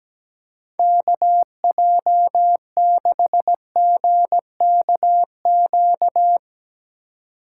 Morse code, heard as a keyed tone, about 18 wpm, 700 Hz. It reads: KJ6GKQ
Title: Speed Racing - USA 2x3 Call Signs 17wpm